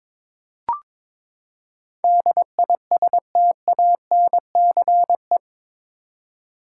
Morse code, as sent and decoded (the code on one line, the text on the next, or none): -.. .. ... - .- -. -.-. .
DISTANCE